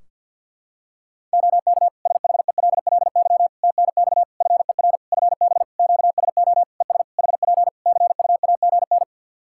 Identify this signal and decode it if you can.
Morse code, heard as a keyed tone, about 52 wpm, 700 Hz. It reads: OK SHELLY TNX FER FB QSO ES HP CUAGN
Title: Practice QSO - Exchange 4 - Ending - 50wpm